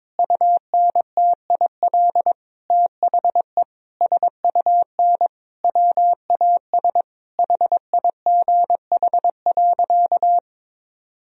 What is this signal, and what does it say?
Morse code, heard as a keyed tone, about 22 wpm, 700 Hz. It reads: UNTIL THE SUN WAS HIGH.
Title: Sentences from Top 1000 Words - Review Part 2 - 22wpm